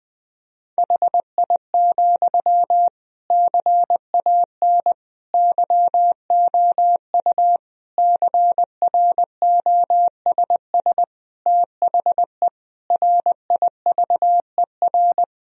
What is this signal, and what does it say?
Morse code, heard as a keyed tone, about 21 wpm, 700 Hz. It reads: HI, CAN YOU CROSS THE RIVER
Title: Sentences from Top 1000 Words - Review Part 3 - 20wpm